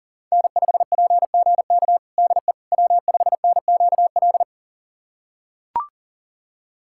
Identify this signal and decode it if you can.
Morse code, heard as a keyed tone, about 40 wpm, 700 Hz. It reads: N5PGK DE W5NQL
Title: USA Callsign DE Callsign - 5 Characters - 40wpm